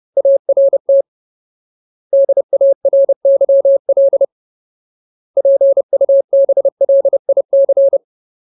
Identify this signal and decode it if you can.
Morse code, heard as a keyed tone, about 30 wpm, 550 Hz. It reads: ART DARYL PUBLIC
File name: Ultimate Word List / 30WPM Ultimate